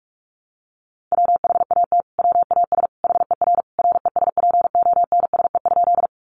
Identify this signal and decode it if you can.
Morse code, heard as a keyed tone, about 44 wpm, 700 Hz. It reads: WHAT WAS HER RESPONSE?
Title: Sentences from Top 1400 Words 45wpm